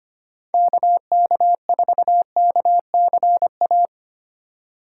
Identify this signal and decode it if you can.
Morse code, heard as a keyed tone, about 32 wpm, 700 Hz. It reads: KK4KCA